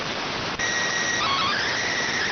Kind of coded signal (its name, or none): SSTV (slow-scan television)